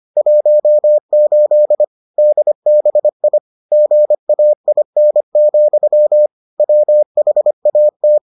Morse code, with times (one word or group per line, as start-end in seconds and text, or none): 0.2-1.8 18
2.2-3.4 DBI
3.7-6.3 GAIN,
6.6-8.2 WHAT